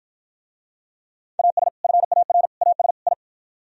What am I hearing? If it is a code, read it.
Morse code, heard as a keyed tone, about 54 wpm, 700 Hz. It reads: AS FAR AS I